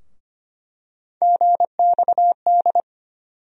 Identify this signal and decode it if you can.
Morse code, heard as a keyed tone, about 25 wpm, 700 Hz. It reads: GXD
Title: USA Call Sign Postfixes - 3 Letters 25wpm